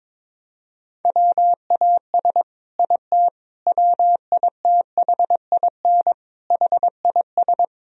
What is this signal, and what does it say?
Morse code, heard as a keyed tone, about 22 wpm, 700 Hz. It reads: WAS IT WITHIN HIS
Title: Sentences from Top 1000 Words - Review Part 4 - 22wpm